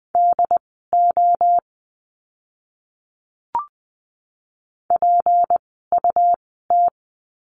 Morse code, heard as a keyed tone, about 20 wpm, spaced as for 14 wpm, 700 Hz. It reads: DO PUT
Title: 206 Lesson 33f - Letter j - Words 20wpm